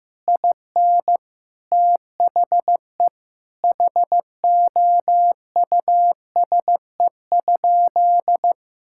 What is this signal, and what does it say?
Morse code, heard as a keyed tone, about 16 wpm, 700 Hz. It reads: IN THE HOUSE?
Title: Sentences from Top 1000 Words - Review Part 5 - 15wpm